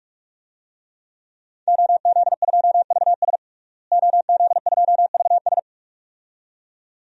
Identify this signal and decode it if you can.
Morse code, heard as a keyed tone, about 46 wpm, 700 Hz. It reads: OZ2VS OZ2VS
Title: DXCC-100 Call Signs - 5 Characters - Repeated 2x - 45wpm